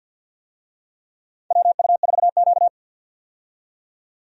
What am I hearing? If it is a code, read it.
Morse code, heard as a keyed tone, about 52 wpm, 700 Hz. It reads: WU4Q
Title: USA Call Signs - 4 Characters - 50wpm